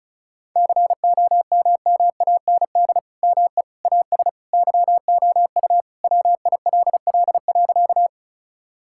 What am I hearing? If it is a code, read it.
Morse code, heard as a keyed tone, about 34 wpm, 700 Hz. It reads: COMMAND ME AS YOU WILL.
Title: Sentences from Top 1600 Words 35wpm